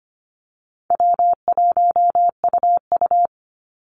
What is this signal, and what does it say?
Morse code, heard as a keyed tone, about 25 wpm, 700 Hz. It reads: W1UU
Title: Easy Call Signs Repeated 2x - 25wpm